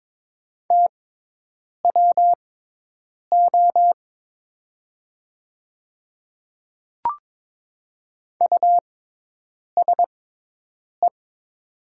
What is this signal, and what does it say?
Morse code, heard as a keyed tone, about 23 wpm, spaced as for 8 wpm, 700 Hz. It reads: TWO USE
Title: 6x Character Spacing - Top 500 Words - 22wpm